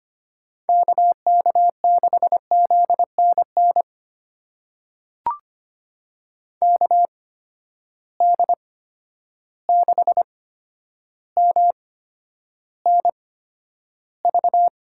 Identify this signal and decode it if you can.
Morse code, heard as a keyed tone, about 25 wpm, spaced as for 11 wpm, 700 Hz. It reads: KK6ZNN KD6MNV
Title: USA Call Signs - 6 Characters - 8x Character Spacing - 25wpm